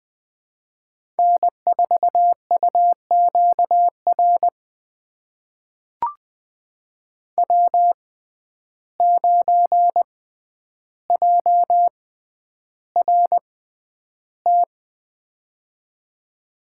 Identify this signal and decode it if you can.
Morse code, heard as a keyed tone, about 20 wpm, spaced as for 10 wpm, 700 Hz. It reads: N4UQR W9JRT
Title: USA Call Signs - 5 Characters - 6x Character Spacing - 20wpm